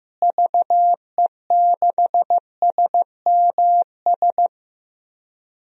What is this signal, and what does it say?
Morse code, heard as a keyed tone, about 16 wpm, 700 Hz. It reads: VE6SMS
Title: DXCC-100 Call Signs - 6 Characters - Repeated 2x - 15wpm